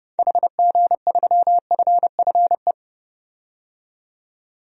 Morse code, heard as a keyed tone, about 30 wpm, 700 Hz. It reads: HG3FFE
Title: DXCC-100 Call Signs - 6 Characters - 30wpm